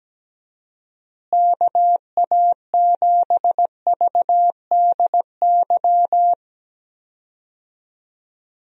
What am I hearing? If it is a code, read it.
Morse code, heard as a keyed tone, about 17 wpm, 700 Hz. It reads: KA7VDY
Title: USA Call Signs - 6 Characters - Repeated 2x - 17wpm